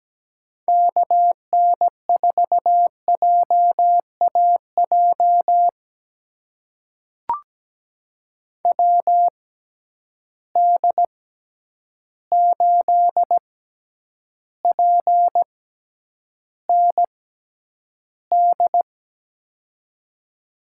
Morse code, heard as a keyed tone, about 17 wpm, spaced as for 9 wpm, 700 Hz. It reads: KN4JAJ WD8PND
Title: USA Call Signs - 6 Characters - 6x Character Spacing - 17wpm